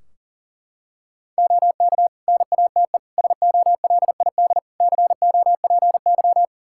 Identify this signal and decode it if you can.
Morse code, heard as a keyed tone, about 40 wpm, 700 Hz. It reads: OK NATE SOLID COPY
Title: Practice QSO - Exchange 3 - More Info - 40wpm